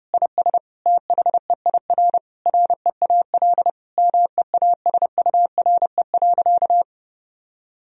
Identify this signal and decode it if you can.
Morse code, heard as a keyed tone, about 32 wpm, 700 Hz. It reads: IS THEIR REAL MEASURE.
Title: Sentences from Top 1000 Words - Review Part 1 - 30wpm